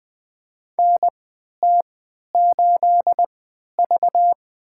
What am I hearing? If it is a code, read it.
Morse code, heard as a keyed tone, about 21 wpm, spaced as for 12 wpm, 700 Hz. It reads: NT8V
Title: USA Call Signs - 3x Character Spacing - 20wpm